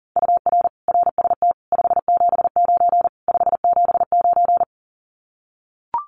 Morse code, heard as a keyed tone, about 40 wpm, 700 Hz. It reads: UR RST 579 579
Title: Common QSO Phrases Encore 40wpm